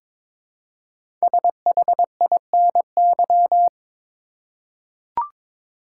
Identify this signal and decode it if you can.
Morse code, heard as a keyed tone, about 22 wpm, 700 Hz. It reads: SHINY
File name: 5 Letter Words - First 3 Letters Given - 22wpm